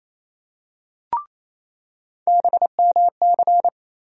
Morse code, heard as a keyed tone, about 28 wpm, 700 Hz. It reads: BMC
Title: USA Call Sign Postfixes - 3 Letters 28wpm